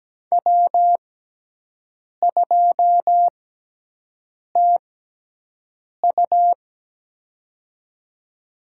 Morse code, heard as a keyed tone, about 17 wpm, spaced as for 6 wpm, 700 Hz. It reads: W2TU
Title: USA Call Signs - 4 Characters - 6x Character Spacing - 17wpm